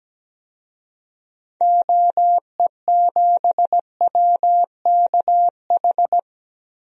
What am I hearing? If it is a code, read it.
Morse code, heard as a keyed tone, about 17 wpm, 700 Hz. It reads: OE7WKH
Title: DXCC-100 Call Signs - 6 Characters - 17wpm